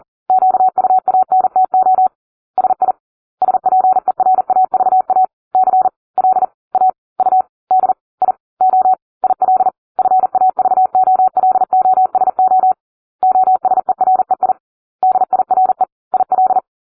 Morse code, heard as a keyed tone, about 40 wpm, spaced as for 35 wpm, 750 Hz. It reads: QUANTO SI SPERAVA CLAUDIO IL FAVOLOSO OSEREI DIRE IL